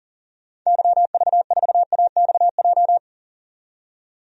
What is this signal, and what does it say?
Morse code, heard as a keyed tone, about 40 wpm, 700 Hz. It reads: YV4AXJ
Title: DXCC-100 Call Signs - 6 Characters - 40wpm